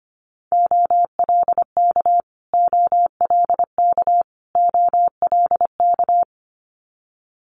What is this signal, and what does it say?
Morse code, heard as a keyed tone, about 25 wpm, 700 Hz. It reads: OLK OLK OLK
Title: Multi-letter Phonemes Repeated 3x 25wpm